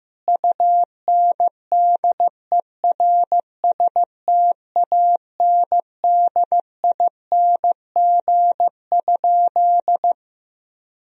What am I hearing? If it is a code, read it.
Morse code, heard as a keyed tone, about 15 wpm, 700 Hz. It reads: UNDERSTANDING?
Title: Sentences from Top 1000 Words - Review Part 3 - 15wpm